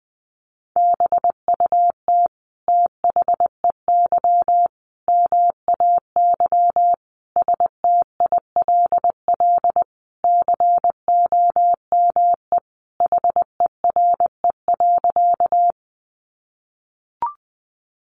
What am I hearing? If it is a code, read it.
Morse code, heard as a keyed tone, about 20 wpm, 700 Hz. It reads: BUT THEY MAY STILL COME HERE.
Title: Sentences from Top 300 Words 20wpm